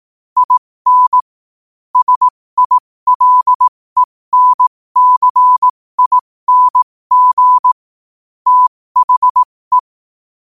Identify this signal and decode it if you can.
Morse code, heard as a keyed tone, about 18 wpm, spaced as for 15 wpm, 1000 Hz. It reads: IN SILENCING THE
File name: qotdfiles / 15wpm